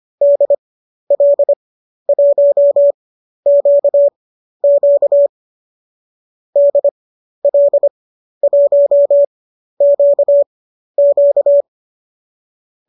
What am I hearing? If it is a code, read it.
Morse code, heard as a keyed tone, about 25 wpm, spaced as for 12 wpm, 590 Hz. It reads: DL1QQ DL1QQ
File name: qso012 12